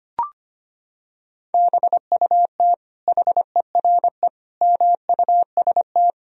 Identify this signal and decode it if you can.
Morse code, heard as a keyed tone, about 25 wpm, 700 Hz. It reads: BUT HERE MUST
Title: Sentences from Top 1000 Words - Review Part 1 - 25wpm